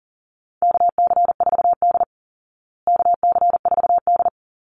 Morse code, heard as a keyed tone, about 40 wpm, 700 Hz. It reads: KC4D KC4D
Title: Call Signs Repeated 3x - 40wpm